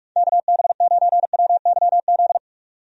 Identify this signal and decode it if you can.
Morse code, heard as a keyed tone, about 43 wpm, 700 Hz. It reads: KB9WYZ
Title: USA Call Signs - 6 Characters - 45wpm